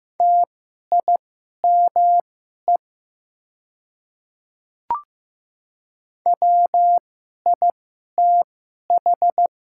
Morse code, heard as a keyed tone, about 15 wpm, spaced as for 11 wpm, 700 Hz. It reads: TIME WITH
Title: 2x Character Spacing - Top 200 Words - 15wpm